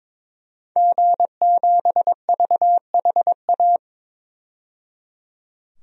Morse code, heard as a keyed tone, about 21 wpm, 700 Hz. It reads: G7VHA
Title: DXCC-100 Call Signs - 5 Characters - 22wpm